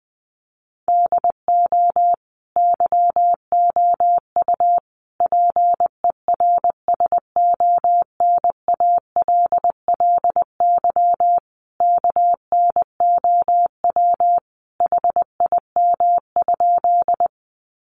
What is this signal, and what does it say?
Morse code, heard as a keyed tone, about 19 wpm, 700 Hz. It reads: DO YOU PERSONALLY KNOW HIM?
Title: Sentences from Top 1400 Words 20wpm